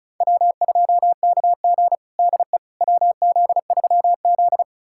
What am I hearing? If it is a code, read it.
Morse code, heard as a keyed tone, about 36 wpm, 700 Hz. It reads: W2KG DE WZ3Z
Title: USA Callsign DE Callsign - 4 Characters - 35wpm